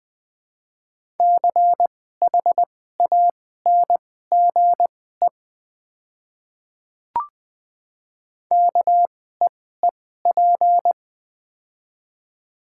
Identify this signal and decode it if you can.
Morse code, heard as a keyed tone, about 20 wpm, spaced as for 14 wpm, 700 Hz. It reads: CHANGE KEEP
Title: 2x Character Spacing - Top 300 Words - 20wpm